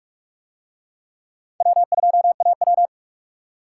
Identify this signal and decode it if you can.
Morse code, heard as a keyed tone, about 42 wpm, 700 Hz. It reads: W2AW